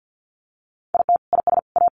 Morse code, heard as a keyed tone, about 50 wpm, 700 Hz. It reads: IT IS A
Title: Sentences from Top 1500 Words 50wpm